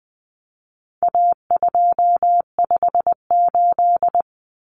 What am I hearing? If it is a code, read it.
Morse code, heard as a keyed tone, about 20 wpm, 700 Hz. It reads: A258
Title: Common Cut Numbers - 4-Digit 20wpm